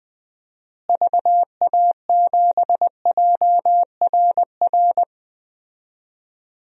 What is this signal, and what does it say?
Morse code, heard as a keyed tone, about 20 wpm, 700 Hz. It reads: VA7JRR
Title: DXCC-100 Call Signs - 6 Characters - Repeated 3x - 20wpm